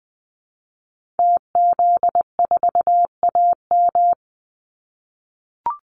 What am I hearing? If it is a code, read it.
Morse code, heard as a keyed tone, about 20 wpm, 700 Hz. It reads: TZ4AM
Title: Call Signs - 20wpm